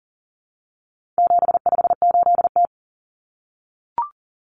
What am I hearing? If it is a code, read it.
Morse code, heard as a keyed tone, about 40 wpm, 700 Hz. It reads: 758T
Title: Common Cut Numbers - 4-Digit 40wpm